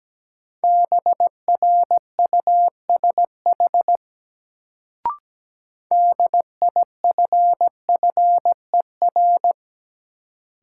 Morse code, heard as a keyed tone, about 17 wpm, 700 Hz. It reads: BRUSH DIFFER
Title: Sentences from Top 1100 Words - New Word Drill Alt - 17wpm